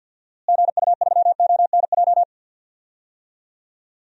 Morse code, heard as a keyed tone, about 49 wpm, 700 Hz. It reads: GU3ONJ
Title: DXCC-100 Call Signs - 6 Characters - Repeated 2x - 50wpm